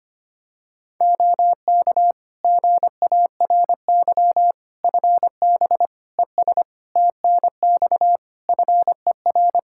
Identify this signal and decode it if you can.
Morse code, heard as a keyed tone, about 25 wpm, 700 Hz. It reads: OK GARY FB ES TNX FER
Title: Practice QSO - Exchange 2 - Info - 25wpm